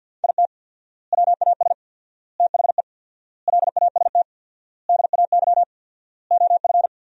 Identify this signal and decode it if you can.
Morse code, heard as a keyed tone, about 50 wpm, 700 Hz. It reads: IT WAS THE LAST DAY OF
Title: 4x Word-Spacing - Sentences from Top 300 Words - 50wpm